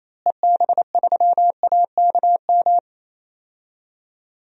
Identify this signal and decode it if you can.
Morse code, heard as a keyed tone, about 28 wpm, 700 Hz. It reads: EB3AKM
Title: DXCC-100 Call Signs - 6 Characters - Repeated 3x - 28wpm